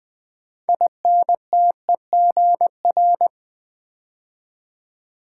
Morse code, heard as a keyed tone, about 20 wpm, 700 Hz. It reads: INTEGR
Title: Build A Word - 3 Letters At A Time - 8-Letter Words - 20wpm